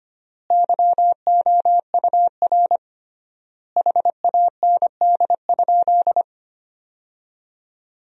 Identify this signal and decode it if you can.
Morse code, heard as a keyed tone, about 25 wpm, 700 Hz. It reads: YOUR HAND?
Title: 3x Word-Spacing - Sentences from Top 200 Words - 25wpm